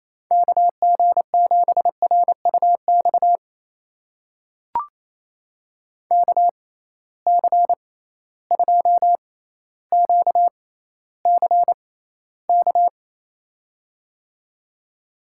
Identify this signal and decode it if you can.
Morse code, heard as a keyed tone, about 28 wpm, spaced as for 14 wpm, 700 Hz. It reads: KG7RUX KC2QCK
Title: USA Call Signs - 6 Characters - 6x Character Spacing - 28wpm